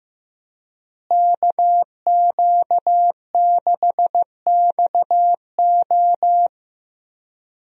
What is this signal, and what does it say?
Morse code, heard as a keyed tone, about 15 wpm, 700 Hz. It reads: KQ6XO
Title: USA Call Signs - 5 Characters - Repeated 2x - 15wpm